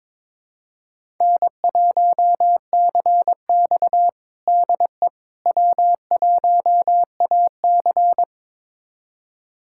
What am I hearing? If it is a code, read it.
Morse code, heard as a keyed tone, about 22 wpm, 700 Hz. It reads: N1CX DE W1AC
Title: USA Callsign DE Callsign - 4 Characters - 22wpm